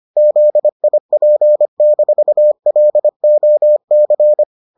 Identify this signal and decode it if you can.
Morse code, heard as a keyed tone, about 25 wpm, 600 Hz. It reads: ZIP-LOC